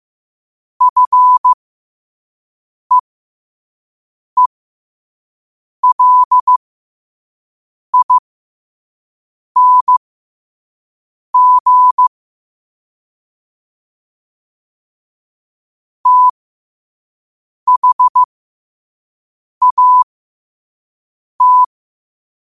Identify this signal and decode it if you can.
Morse code, heard as a keyed tone, about 15 wpm, spaced as for 5 wpm, 1000 Hz. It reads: FEELING THAT
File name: qotdfiles / 5wpm